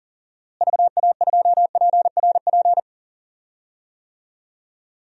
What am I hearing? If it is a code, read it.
Morse code, heard as a keyed tone, about 40 wpm, 700 Hz. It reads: VA2PRP
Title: DXCC-100 Call Signs - 6 Characters - Repeated 2x - 40wpm